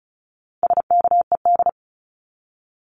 Morse code, heard as a keyed tone, about 34 wpm, 700 Hz. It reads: SKED
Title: Abbreviations 35wpm